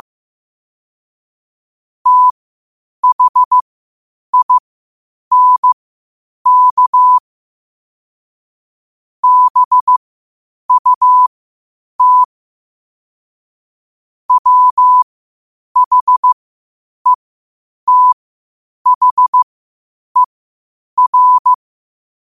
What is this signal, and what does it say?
Morse code, heard as a keyed tone, about 15 wpm, spaced as for 8 wpm, 1000 Hz. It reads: THINK BUT WHETHER
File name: qotdfiles / 8wpm